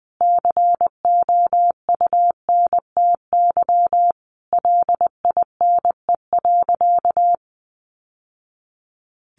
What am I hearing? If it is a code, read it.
Morse code, heard as a keyed tone, about 20 wpm, 700 Hz. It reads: COUNTY LINE.
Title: Sentences from Top 1200 Words 20wpm